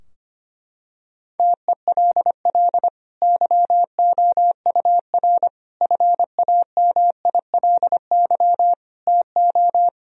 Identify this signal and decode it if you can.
Morse code, heard as a keyed tone, about 25 wpm, 700 Hz. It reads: TELL YOUR FAMILY TO
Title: Sentences from Top 1000 Words - Review Part 2 - 25wpm